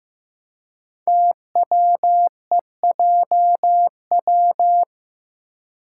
Morse code, heard as a keyed tone, about 15 wpm, 700 Hz. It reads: TWEJW
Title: Code Groups - 5 Letters - 15wpm